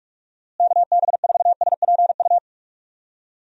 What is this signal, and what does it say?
Morse code, heard as a keyed tone, about 46 wpm, 700 Hz. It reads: KB4SPU
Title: USA Call Signs - 6 Characters - Repeated 3x - 45wpm